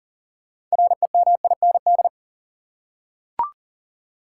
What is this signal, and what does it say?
Morse code, heard as a keyed tone, about 40 wpm, 700 Hz. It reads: REMIND
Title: Six Letter Words - 40wpm